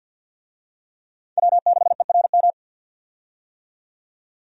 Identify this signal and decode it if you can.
Morse code, heard as a keyed tone, about 52 wpm, 700 Hz. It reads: W6ERM
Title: USA Call Signs - 5 Characters - 50wpm